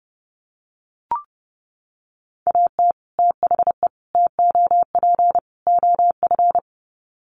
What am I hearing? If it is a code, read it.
Morse code, heard as a keyed tone, about 30 wpm, 700 Hz. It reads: AT THE TOP OF
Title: Sets of 4 Words 30wpm